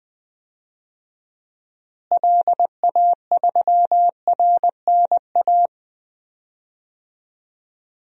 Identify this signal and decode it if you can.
Morse code, heard as a keyed tone, about 20 wpm, 700 Hz. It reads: LA3RNA
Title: DXCC-100 Call Signs - 6 Characters - Repeated 2x - 20wpm